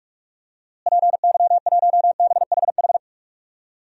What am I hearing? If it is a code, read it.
Morse code, heard as a keyed tone, about 46 wpm, 700 Hz. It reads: PY1BHH
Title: DXCC-100 Call Signs - 6 Characters - Repeated 2x - 45wpm